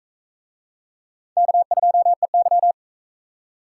Morse code, heard as a keyed tone, about 38 wpm, 700 Hz. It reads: K2EY